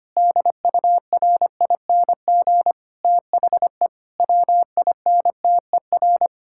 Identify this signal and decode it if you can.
Morse code, heard as a keyed tone, about 25 wpm, 700 Hz. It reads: DURING THE WINTER
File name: practice / 25wpm / Difficult Sentences with Hints - 25wpm